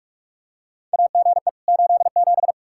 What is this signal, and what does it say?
Morse code, heard as a keyed tone, about 46 wpm, 700 Hz. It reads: AGE 87
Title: K1USN SST QSO Practice 45wpm